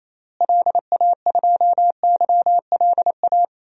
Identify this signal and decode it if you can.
Morse code, heard as a keyed tone, about 28 wpm, 700 Hz. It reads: LA2YLA
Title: DXCC-100 Call Signs - 6 Characters - Repeated 3x - 28wpm